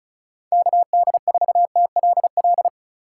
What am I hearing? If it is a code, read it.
Morse code, heard as a keyed tone, about 34 wpm, 700 Hz. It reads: KD4TLL
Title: USA Call Signs - 6 Characters - 3x Character Spacing - 35wpm